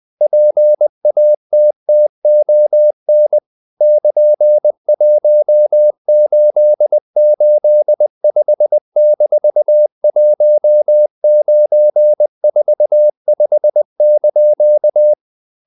Morse code, heard as a keyed tone, about 20 wpm, 600 Hz. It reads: PATTON <KN>1885-1945)
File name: CW / Practice / EnglishQuotations0320